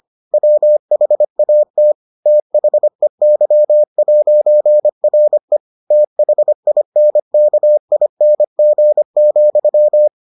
Morse code, heard as a keyed tone, about 25 wpm, 600 Hz. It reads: WHAT THEY'RE THINKING,